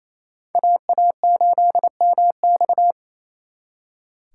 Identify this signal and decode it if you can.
Morse code, heard as a keyed tone, about 35 wpm, 700 Hz. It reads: AA8MX
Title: Speed Racing - USA 2x2 Call Signs 28wpm